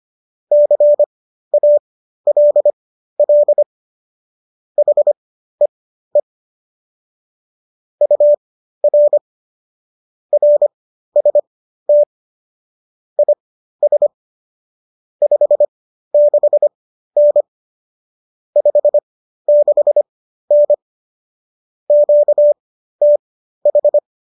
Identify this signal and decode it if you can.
Morse code, heard as a keyed tone, about 25 wpm, spaced as for 13 wpm, 590 Hz. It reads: CALL HEE UR RST IS 56N 56N QTH